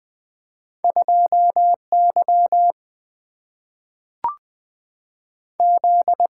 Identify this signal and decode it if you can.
Morse code, heard as a keyed tone, about 20 wpm, 700 Hz. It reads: 2Y Z
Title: Single-Letter Single-Number 20wpm